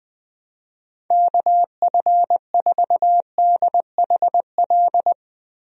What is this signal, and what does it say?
Morse code, heard as a keyed tone, about 20 wpm, 700 Hz. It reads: KF4DHL
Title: USA Call Signs - 6 Characters - Repeated 2x - 20wpm